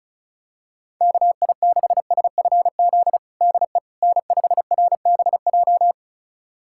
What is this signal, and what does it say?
Morse code, heard as a keyed tone, about 34 wpm, 700 Hz. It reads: KI6SFZ DE N5RBJ
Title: USA Callsign DE Callsign 35wpm